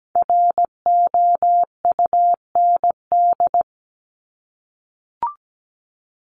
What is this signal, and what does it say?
Morse code, heard as a keyed tone, about 17 wpm, 700 Hz. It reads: ROUND
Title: Top 200 Words Repeated 17wpm